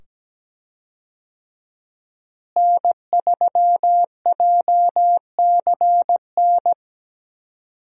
Morse code, heard as a keyed tone, about 17 wpm, 700 Hz. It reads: N3JCN
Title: USA Call Signs - 5 Characters - Repeated 2x - 17wpm